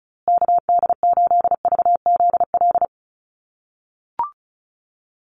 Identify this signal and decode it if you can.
Morse code, heard as a keyed tone, about 34 wpm, 700 Hz. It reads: KD8VZL
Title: USA Call Signs Repeated 3x 35wpm